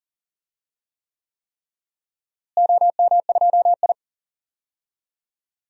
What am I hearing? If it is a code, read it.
Morse code, heard as a keyed tone, about 40 wpm, 700 Hz. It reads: OM2I
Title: DXCC-100 Call Signs - 4 Characters - Repeated 3x - 40wpm